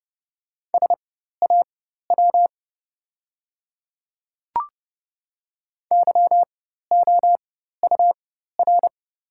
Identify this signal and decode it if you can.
Morse code, heard as a keyed tone, about 30 wpm, spaced as for 14 wpm, 700 Hz. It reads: SAW YOUR
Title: 4x Character Spacing - Top 300 Words - 30wpm